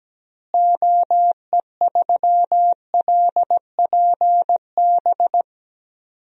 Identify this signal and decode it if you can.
Morse code, heard as a keyed tone, about 17 wpm, 700 Hz. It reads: OE3LPB
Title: DXCC-100 Call Signs - 6 Characters - Repeated 2x - 17wpm